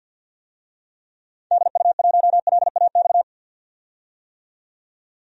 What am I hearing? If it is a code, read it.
Morse code, heard as a keyed tone, about 52 wpm, 700 Hz. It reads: DU1LAX
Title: DXCC-100 Call Signs - 6 Characters - 50wpm